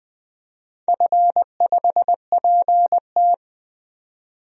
Morse code, heard as a keyed tone, about 20 wpm, 700 Hz. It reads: F5PT
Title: DXCC-100 Call Signs - 4 Characters - 20wpm